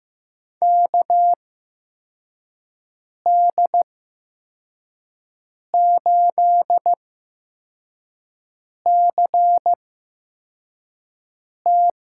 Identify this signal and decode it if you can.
Morse code, heard as a keyed tone, about 15 wpm, spaced as for 4 wpm, 700 Hz. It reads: KD8CT
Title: USA Call Signs - 6 Characters - 8x Character Spacing - 15wpm